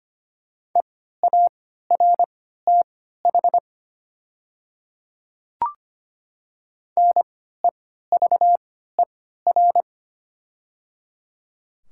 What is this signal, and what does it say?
Morse code, heard as a keyed tone, about 25 wpm, spaced as for 14 wpm, 700 Hz. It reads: EARTH NEVER
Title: 3x Character Spacing - Top 300 Words - 25wpm